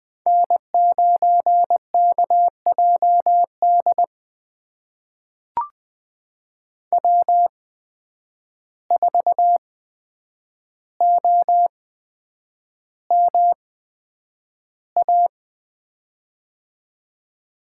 Morse code, heard as a keyed tone, about 20 wpm, spaced as for 9 wpm, 700 Hz. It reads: N9KJD W4OMA
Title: USA Call Signs - 5 Characters - 8x Character Spacing - 20wpm